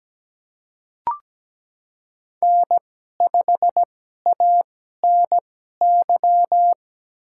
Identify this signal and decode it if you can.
Morse code, heard as a keyed tone, about 17 wpm, spaced as for 12 wpm, 700 Hz. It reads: N5ANY
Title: USA Call Signs - 5 Characters - 2x Character Spacing - 17wpm